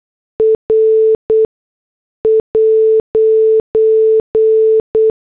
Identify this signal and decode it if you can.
Morse code, heard as a keyed tone, about 8 wpm, spaced as for 6 wpm, 440 Hz. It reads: R'
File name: morse / headlines